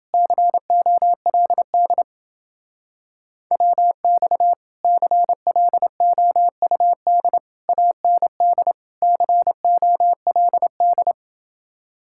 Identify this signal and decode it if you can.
Morse code, heard as a keyed tone, about 29 wpm, 700 Hz. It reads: COLD WX CLOUD AND COLD